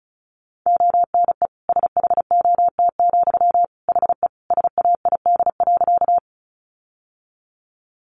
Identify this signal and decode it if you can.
Morse code, heard as a keyed tone, about 34 wpm, 700 Hz. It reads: ONE SHOT, HE SAID.
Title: Sentences from Top 1100 Words 35wpm